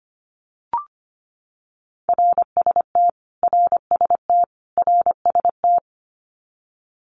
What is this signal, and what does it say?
Morse code, heard as a keyed tone, about 25 wpm, 700 Hz. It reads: RST RST RST
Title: Abbreviations Repeated 3x - 25wpm